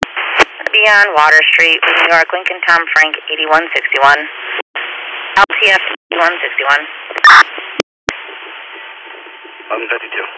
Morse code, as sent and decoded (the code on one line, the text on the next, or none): none